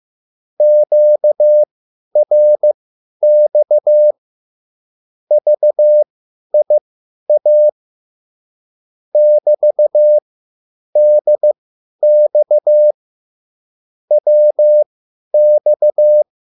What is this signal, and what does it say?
Morse code, heard as a keyed tone, about 15 wpm, spaced as for 10 wpm, 600 Hz. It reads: QRX VIA <BT>DX WX